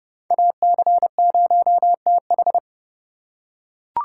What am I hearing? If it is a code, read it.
Morse code, heard as a keyed tone, about 27 wpm, 700 Hz. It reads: AC0TH